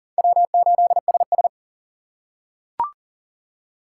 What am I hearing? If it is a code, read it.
Morse code, heard as a keyed tone, about 40 wpm, 700 Hz. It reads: W8SS
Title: USA Call Signs - 4 Characters - 4x Character Spacing - 40wpm